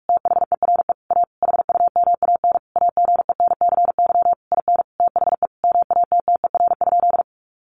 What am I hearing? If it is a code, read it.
Morse code, heard as a keyed tone, about 44 wpm, 700 Hz. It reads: THERE A HUMAN AGENCY IN THE MATTER?
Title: Sentences from Top 1300 Words 45wpm